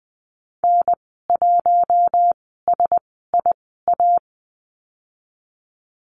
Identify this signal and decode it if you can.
Morse code, heard as a keyed tone, about 20 wpm, spaced as for 14 wpm, 700 Hz. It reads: N1SIA
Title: 069 Lesson 12g - Letter d - Call Signs 20wpm